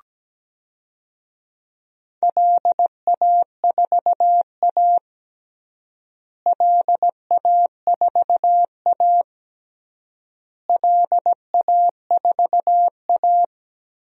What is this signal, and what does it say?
Morse code, heard as a keyed tone, about 17 wpm, 700 Hz. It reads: LA4A LA4A LA4A
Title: DXCC-100 Call Signs - 4 Characters - Repeated 3x - 17wpm